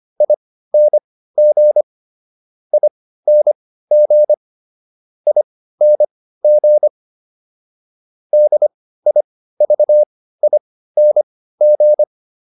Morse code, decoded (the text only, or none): ING ING ING DIVING